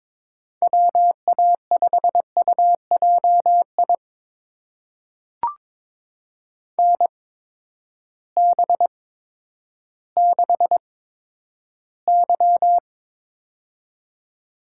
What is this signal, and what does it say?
Morse code, heard as a keyed tone, about 22 wpm, spaced as for 11 wpm, 700 Hz. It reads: WA5UJI NB6Y
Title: USA Call Signs - 8x Character Spacing - 22wpm